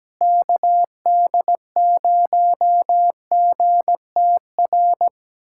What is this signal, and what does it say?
Morse code, heard as a keyed tone, about 17 wpm, 700 Hz. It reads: KD0GTR
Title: USA Callsign DE Callsign - 6 Characters - 17wpm